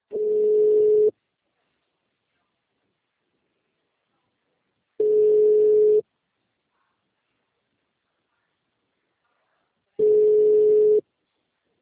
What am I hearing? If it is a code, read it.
Ringback tone (Europe, 425 Hz).
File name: out- -2001- - - .97540